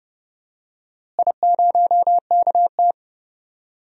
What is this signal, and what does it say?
Morse code, heard as a keyed tone, about 30 wpm, 700 Hz. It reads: I0KT
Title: DXCC-100 Call Signs - 4 Characters - Repeated 3x - 30wpm